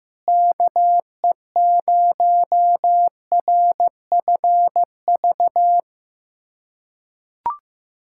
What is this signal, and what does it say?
Morse code, heard as a keyed tone, about 15 wpm, 700 Hz. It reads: KE0RFV
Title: USA Callsign DE Callsign - 6 Characters - 15wpm